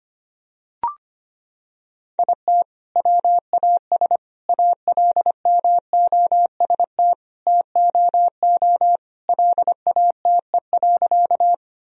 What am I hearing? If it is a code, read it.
Morse code, heard as a keyed tone, about 25 wpm, 700 Hz. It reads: IT WAS ALMOST TOO LATE.
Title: Sentences from Top 1000 Words - Review Part 4 - 25wpm